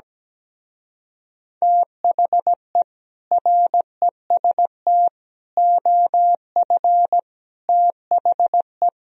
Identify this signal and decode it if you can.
Morse code, heard as a keyed tone, about 17 wpm, 700 Hz. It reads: THE REST OF THE
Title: Speed-Racing - Sets of 5 Words 17wpm